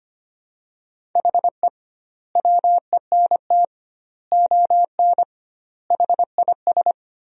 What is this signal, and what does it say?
Morse code, heard as a keyed tone, about 25 wpm, 700 Hz. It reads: HE WENT ON HIS
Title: 2x Word-Spacing - Sentences from Top 200 Words - 25wpm